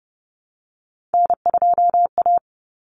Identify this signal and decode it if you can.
Morse code, heard as a keyed tone, about 30 wpm, 700 Hz. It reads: N2A
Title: Common Cut Numbers - 3-Digit 30wpm